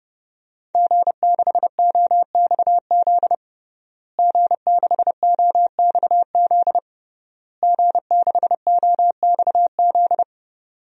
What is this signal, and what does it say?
Morse code, heard as a keyed tone, about 31 wpm, 700 Hz. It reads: G6OXZ G6OXZ G6OXZ
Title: DXCC-100 Call Signs - 5 Characters - Repeated 3x - 30wpm